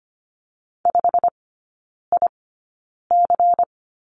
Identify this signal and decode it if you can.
Morse code, heard as a keyed tone, about 25 wpm, spaced as for 9 wpm, 700 Hz. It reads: 5IC
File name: course / 25:10wpm / 093 Lesson 16e - Letter c - Three Characters 25wpm